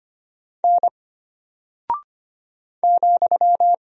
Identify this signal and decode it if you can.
Morse code, heard as a keyed tone, about 25 wpm, 700 Hz. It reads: N ,
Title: Single Character (Letter:Number:Punctuation:Prosign) Alt 25wpm